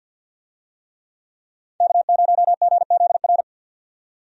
Morse code, heard as a keyed tone, about 52 wpm, 700 Hz. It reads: K0GZR
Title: USA Call Signs - 5 Characters - 50wpm